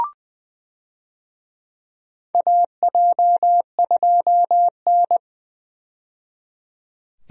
Morse code, heard as a keyed tone, about 20 wpm, 700 Hz. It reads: AJ2N
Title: USA Call Signs - 4 Characters - 20wpm